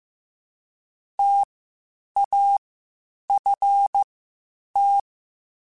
Morse code, heard as a keyed tone, about 15 wpm, spaced as for 8 wpm, 800 Hz. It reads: TAFT